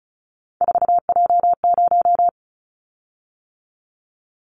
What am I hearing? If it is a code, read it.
Morse code, heard as a keyed tone, about 34 wpm, 700 Hz. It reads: 4J0
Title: Callsign Prefixes for Top 100 DXCC - 35wpm